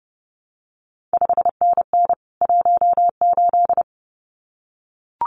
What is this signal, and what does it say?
Morse code, heard as a keyed tone, about 30 wpm, 700 Hz. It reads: 5NN 18
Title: CQ WW DX Contest Exchange Encore - 30wpm